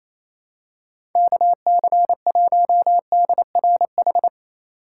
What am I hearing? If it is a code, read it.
Morse code, heard as a keyed tone, about 28 wpm, 700 Hz. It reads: KC1DRH
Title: USA Call Signs - 6 Characters - 2x Character Spacing - 28wpm